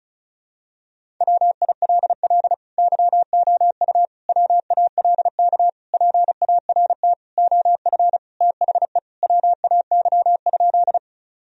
Morse code, heard as a keyed tone, about 36 wpm, 700 Hz. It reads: WILL YOU WALK PART OF THE WAY?
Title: Sentences from Top 1000 Words - Review Part 1 - 35wpm